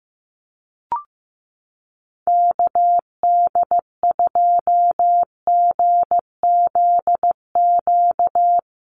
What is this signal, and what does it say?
Morse code, heard as a keyed tone, about 15 wpm, 700 Hz. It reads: KD2GZQ
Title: USA Call Signs Repeated 3x 15wpm